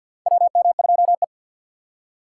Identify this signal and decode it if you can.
Morse code, heard as a keyed tone, about 51 wpm, 700 Hz. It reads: WM2E
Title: USA Call Signs - 4 Characters - 2x Character Spacing - 50wpm